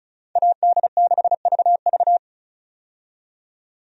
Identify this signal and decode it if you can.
Morse code, heard as a keyed tone, about 36 wpm, 700 Hz. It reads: AD6VV
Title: USA Call Signs - 5 Characters - 35wpm